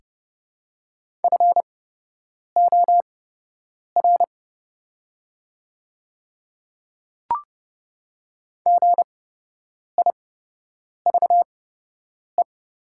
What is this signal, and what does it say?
Morse code, heard as a keyed tone, about 30 wpm, spaced as for 8 wpm, 700 Hz. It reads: FOR GIVE
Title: 8x Character Spacing - Top 300 Words - 30wpm